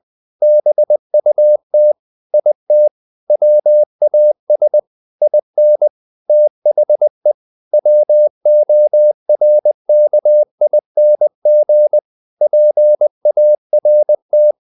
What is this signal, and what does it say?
Morse code, heard as a keyed tone, about 20 wpm, 600 Hz. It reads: BUT IT WAS IN THE WORKING PART
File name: the old man and the sea / the old man and the sea part 1 20wpm